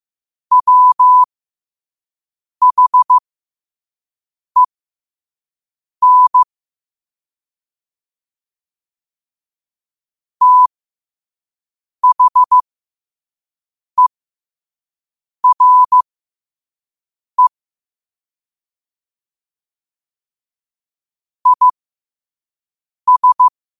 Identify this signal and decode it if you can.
Morse code, heard as a keyed tone, about 15 wpm, spaced as for 5 wpm, 1000 Hz. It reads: WHEN THERE IS
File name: qotdfiles / 5wpm